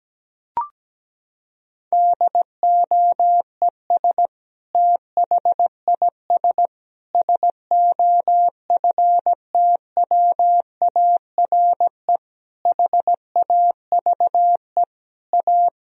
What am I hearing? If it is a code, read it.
Morse code, heard as a keyed tone, about 17 wpm, 700 Hz. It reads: DOES THIS SOFTWARE HAVE A
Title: Sentences from Top 1000 Words - Review Part 3 - 17wpm